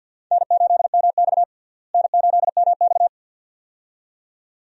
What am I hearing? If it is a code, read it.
Morse code, heard as a keyed tone, about 51 wpm, 700 Hz. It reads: N8MX N8MX
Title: USA Call Signs - 4 Characters - Repeated 2x - 50wpm